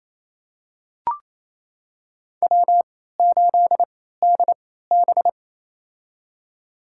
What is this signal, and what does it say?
Morse code, heard as a keyed tone, about 28 wpm, spaced as for 16 wpm, 700 Hz. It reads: W8DB
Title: USA Call Signs - 4 Characters - 3x Character Spacing - 28wpm